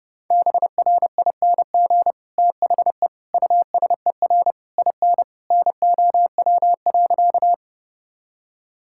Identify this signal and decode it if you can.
Morse code, heard as a keyed tone, about 29 wpm, 700 Hz. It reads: BRING THE USER IN NOW.
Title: Sentences from Top 1000 Words - Review Part 5 - 30wpm